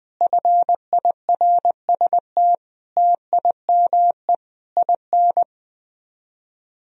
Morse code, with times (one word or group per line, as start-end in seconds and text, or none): 0.2-2.5 FIRST
3.0-4.3 TIME
4.8-5.4 IN